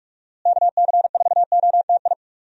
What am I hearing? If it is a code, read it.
Morse code, heard as a keyed tone, about 41 wpm, 700 Hz. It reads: KC4OTI
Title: USA Call Signs - 6 Characters - Repeated 2x - 45wpm